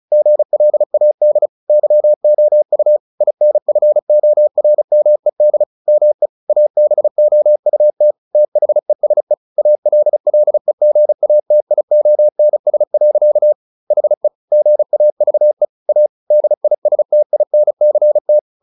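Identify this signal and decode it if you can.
Morse code, heard as a keyed tone, about 34 wpm, 600 Hz. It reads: GLAD YOU INFORMED ME ABOUT THESE ALLEGATIONS. HE GAVE A DISTINCT
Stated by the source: Morse practice at 35 wpm from CW med Gnister, sending Ken Follett's novel Never in English.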